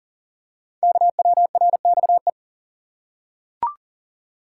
Morse code, heard as a keyed tone, about 41 wpm, 700 Hz. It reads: KWRXE
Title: Code Groups - 5 Letters - 40wpm